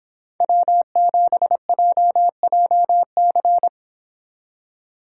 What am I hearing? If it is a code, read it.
Morse code, heard as a keyed tone, about 26 wpm, 700 Hz. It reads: W7JJC